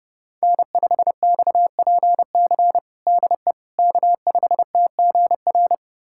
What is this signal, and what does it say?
Morse code, heard as a keyed tone, about 32 wpm, 700 Hz. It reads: N5XPC DE K5TGR
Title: USA Callsign DE Callsign - 5 Characters - 30wpm